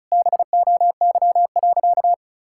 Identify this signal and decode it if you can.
Morse code, heard as a keyed tone, about 36 wpm, 700 Hz. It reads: BOY.
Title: Sentences from Top 1000 Words - Review Part 5 - 35wpm